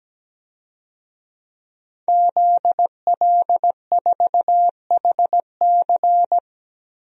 Morse code, heard as a keyed tone, about 17 wpm, 700 Hz. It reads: ZL4HC
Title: DXCC-100 Call Signs - 5 Characters - 17wpm